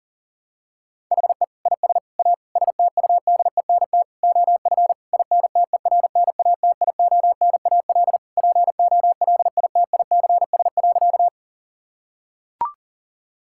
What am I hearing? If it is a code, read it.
Morse code, heard as a keyed tone, about 40 wpm, 700 Hz. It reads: HE IS A STUDENT OF INTERNATIONAL POLITICS.
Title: Sentences from Top 1000 Words - Review Part 6 - 40wpm